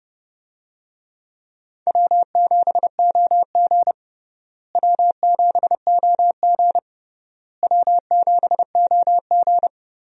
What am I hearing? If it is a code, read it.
Morse code, heard as a keyed tone, about 32 wpm, 700 Hz. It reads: W7OG W7OG W7OG
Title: USA Call Signs - 4 Characters - Repeated 3x - 30wpm